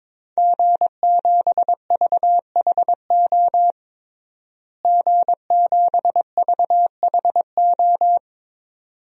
Morse code, heard as a keyed tone, about 22 wpm, 700 Hz. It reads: G7VHO G7VHO
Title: DXCC-100 Call Signs - 5 Characters - Repeated 2x - 22wpm